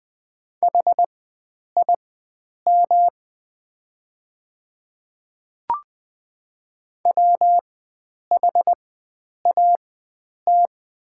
Morse code, heard as a keyed tone, about 20 wpm, spaced as for 9 wpm, 700 Hz. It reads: HIM WHAT
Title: 4x Character Spacing - Top 100 Words - 20wpm